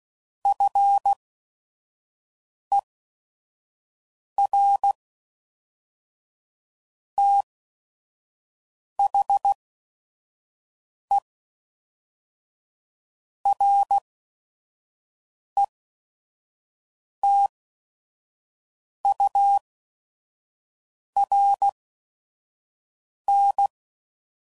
Morse code, heard as a keyed tone, about 16 wpm, spaced as for 5 wpm, 800 Hz. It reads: FER THE RETURN
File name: CW Code Groups @5WPM / QSOs / kc7haa